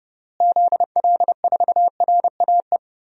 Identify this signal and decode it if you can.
Morse code, heard as a keyed tone, about 32 wpm, 700 Hz. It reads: ZL4RAE
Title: DXCC-100 Call Signs - 6 Characters - Repeated 3x - 30wpm